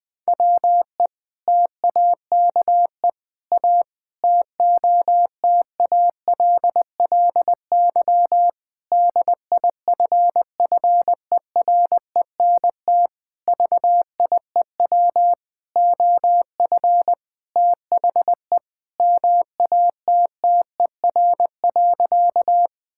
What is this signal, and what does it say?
Morse code, heard as a keyed tone, about 20 wpm, 700 Hz. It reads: WE TAKE A TOTALLY DIFFERENT VIEW OF THE MATTER.
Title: Sentences from Top 1600 Words 20wpm